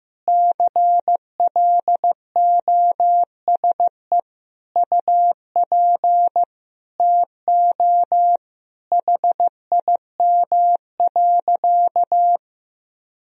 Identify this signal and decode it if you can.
Morse code, heard as a keyed tone, about 16 wpm, 700 Hz. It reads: CLOSE UP TO HIM.
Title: Sentences from Top 1000 Words - Review Part 1 - 15wpm